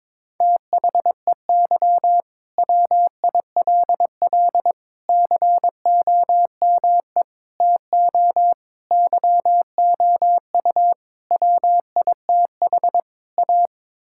Morse code, heard as a keyed tone, about 22 wpm, 700 Hz. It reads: THEY WILL COME TO YOU WITH A
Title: Sentences from Top 1600 Words 22wpm